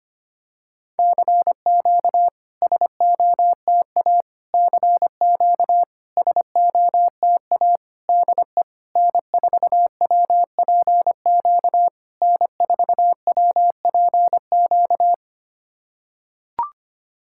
Morse code, heard as a keyed tone, about 26 wpm, 700 Hz. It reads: CQ SOTA CQ SOTA DE N4WPQ N4WPQ
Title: SOTA QSO Practice 25wpm